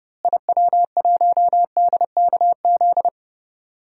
Morse code, heard as a keyed tone, about 30 wpm, 700 Hz. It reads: IW1DKZ
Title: DXCC-100 Call Signs - 6 Characters - Repeated 3x - 30wpm